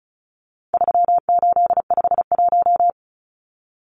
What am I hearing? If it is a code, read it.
Morse code, heard as a keyed tone, about 34 wpm, 700 Hz. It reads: 3851
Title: Numbers - 4-Digit 35wpm